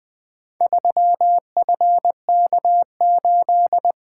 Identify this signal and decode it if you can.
Morse code, heard as a keyed tone, about 20 wpm, 700 Hz. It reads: 3FK8
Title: Four Letter-Number Combinations - 20wpm